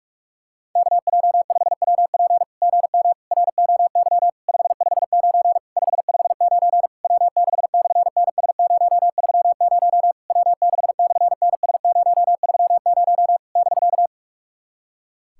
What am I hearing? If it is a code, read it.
Morse code, heard as a keyed tone, about 46 wpm, 700 Hz. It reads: KJ5WP GM ROY 559 559 W6/NS030 W6/NS030 <BK>
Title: SOTA QSO Practice 45wpm